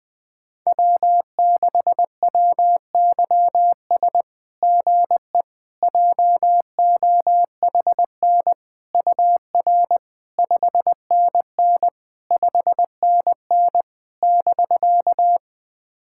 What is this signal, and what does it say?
Morse code, heard as a keyed tone, about 20 wpm, 700 Hz. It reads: W6WYS GE JOHN UR 5NN 5NN <BK>
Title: SOTA QSO Practice 20wpm